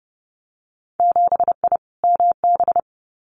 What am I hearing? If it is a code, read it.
Morse code, heard as a keyed tone, about 30 wpm, 700 Hz. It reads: 7I MB
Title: Winter Field Day Exchanges Encore 30wpm